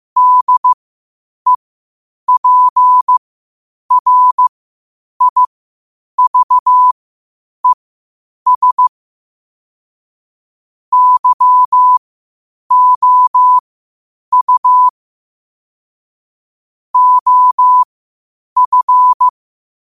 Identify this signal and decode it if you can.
Morse code, heard as a keyed tone, about 15 wpm, spaced as for 8 wpm, 1000 Hz. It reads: DEPRIVES YOU OF
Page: qotd-at-8-wpm